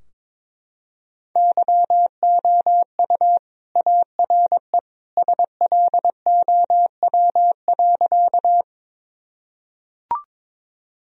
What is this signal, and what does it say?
Morse code, heard as a keyed tone, about 21 wpm, 700 Hz. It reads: YOU ARE SLOW.
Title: Sentences from Top 1000 Words - Review Part 2 - 22wpm